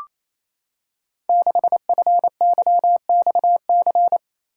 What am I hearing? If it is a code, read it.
Morse code, heard as a keyed tone, about 27 wpm, 700 Hz. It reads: 6FYXC
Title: Code Groups - 5 Letter:Numbers - 28wpm